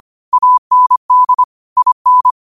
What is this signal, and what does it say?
Morse code, heard as a keyed tone, about 25 wpm, 1000 Hz. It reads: AND IN